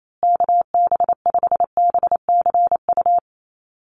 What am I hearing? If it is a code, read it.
Morse code, heard as a keyed tone, about 28 wpm, 700 Hz. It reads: KB5BCU
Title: USA Call Signs Repeated 2x 28wpm